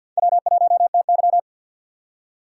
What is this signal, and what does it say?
Morse code, heard as a keyed tone, about 49 wpm, 700 Hz. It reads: W1TY
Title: USA Call Signs - 4 Characters - 6x Character Spacing - 50wpm